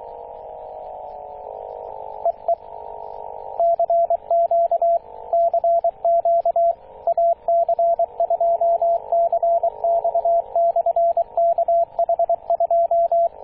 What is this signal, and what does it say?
Morse code, heard as a keyed tone, about 23 wpm, 680 Hz, with radio noise. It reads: EE CQ CQ AC2CX/KH2
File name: posts / 2016-june-26-0520z-ac2cx-kh2-21-027mhz-cw